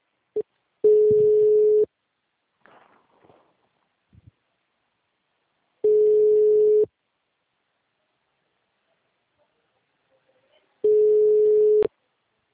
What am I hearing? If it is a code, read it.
Ringback tone (Europe, 425 Hz).
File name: out- -2020- - - .45966